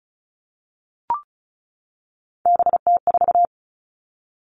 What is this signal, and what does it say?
Morse code, heard as a keyed tone, about 36 wpm, 700 Hz. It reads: BT4
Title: Three Challenging Letter-Number Combinations 35wpm